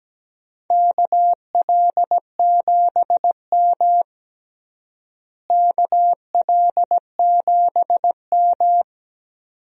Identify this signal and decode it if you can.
Morse code, heard as a keyed tone, about 17 wpm, 700 Hz. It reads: KL7M KL7M
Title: DXCC-100 Call Signs - 4 Characters - Repeated 2x - 17wpm